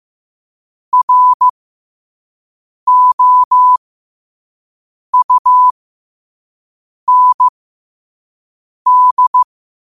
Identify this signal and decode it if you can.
Morse code, heard as a keyed tone, about 15 wpm, spaced as for 5 wpm, 1000 Hz. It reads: ROUND
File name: qotdfiles / 5wpm